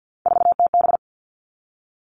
Morse code, heard as a keyed tone, about 50 wpm, 700 Hz. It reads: 4TB
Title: All Cut Numbers - 3-Digit 50wpm